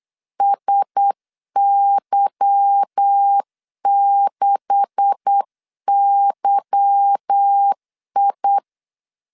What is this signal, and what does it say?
Morse code, heard as a keyed tone, about 8 wpm, 800 Hz. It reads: SY6YI
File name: CW / DET / 1 9 1500C S A 8wpm 5637